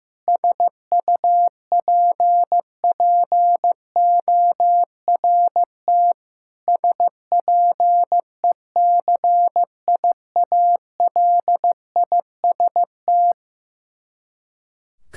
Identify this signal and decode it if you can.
Morse code, heard as a keyed tone, about 15 wpm, 700 Hz. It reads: SUPPORT SPECIALIST
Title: Practice QSO - Exchange 3 - More Info - 15wpm